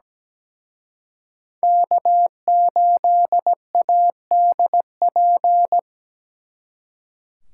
Morse code, heard as a keyed tone, about 20 wpm, 700 Hz. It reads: K8ADP
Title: Speed Racing - USA 1x3 Call Signs 17wpm